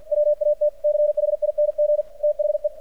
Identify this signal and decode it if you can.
Morse code, heard as a keyed tone, about 50 wpm, 600 Hz, with radio noise. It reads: WATCHING THE
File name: archived / CRSnet